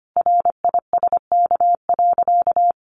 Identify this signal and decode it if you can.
Morse code, heard as a keyed tone, about 25 wpm, 700 Hz. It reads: RISK.
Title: Sentences from Top 600 Words 25wpm